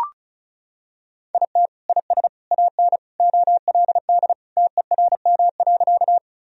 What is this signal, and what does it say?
Morse code, heard as a keyed tone, about 36 wpm, 700 Hz. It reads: IT IS AN OLD TERM.
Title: Sentences from Top 1000 Words - Review Part 4 - 35wpm